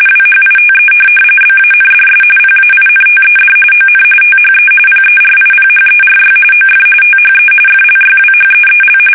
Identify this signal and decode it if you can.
RTTY, about 45.45 baud. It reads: WHITE BANDS PAINTED ON THE PERIPHERY OF THE FLYWHEEL